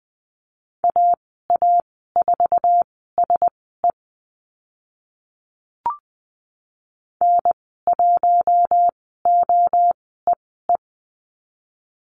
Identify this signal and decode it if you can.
Morse code, heard as a keyed tone, about 20 wpm, spaced as for 14 wpm, 700 Hz. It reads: AA4SE N1OEE
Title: 048 Lesson 9g - Number 4 - Call Signs 20wpm